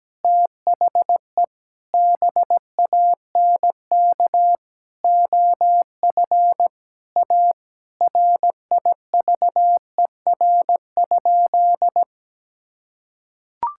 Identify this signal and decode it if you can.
Morse code, heard as a keyed tone, about 17 wpm, 700 Hz. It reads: THE BANK OF A RIVER?
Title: Sentences from Top 1000 Words - Review Part 6 - 17wpm